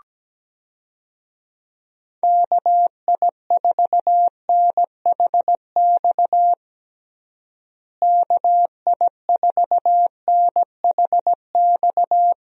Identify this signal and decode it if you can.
Morse code, heard as a keyed tone, about 17 wpm, 700 Hz. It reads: KI4NHX KI4NHX
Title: USA Call Signs - 6 Characters - Repeated 2x - 17wpm